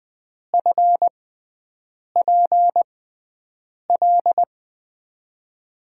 Morse code, heard as a keyed tone, about 20 wpm, spaced as for 7 wpm, 700 Hz. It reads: FPL
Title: USA Call Signs - 6 Characters - 6x Character Spacing - 20wpm